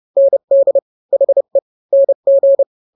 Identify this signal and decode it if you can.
Morse code, heard as a keyed tone, about 30 wpm, 550 Hz. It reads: ND HE NG
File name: n-grams / 2-grams 30 25 WPM